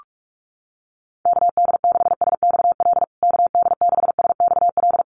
Morse code, heard as a keyed tone, about 44 wpm, 700 Hz. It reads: KD6SXL KD6SXL
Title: USA Call Signs Repeated 2x 45wpm